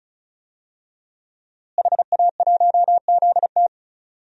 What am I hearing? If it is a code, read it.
Morse code, heard as a keyed tone, about 36 wpm, 700 Hz. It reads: HA1ZT
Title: DXCC-100 Call Signs - 5 Characters - Repeated 2x - 35wpm